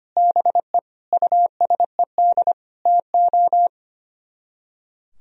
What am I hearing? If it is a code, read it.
Morse code, heard as a keyed tone, about 31 wpm, 700 Hz. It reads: BE USED TO
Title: Speed-Racing - Sets of 4 Words 25wpm